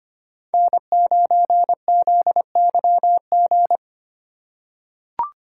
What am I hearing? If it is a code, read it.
Morse code, heard as a keyed tone, about 25 wpm, 700 Hz. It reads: N9ZYG
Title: USA Call Signs - 5 Characters - 6x Character Spacing - 25wpm